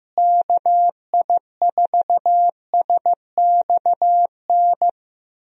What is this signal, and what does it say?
Morse code, heard as a keyed tone, about 15 wpm, 700 Hz. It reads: KI4SXN
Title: USA Call Signs - 6 Characters - Repeated 3x - 15wpm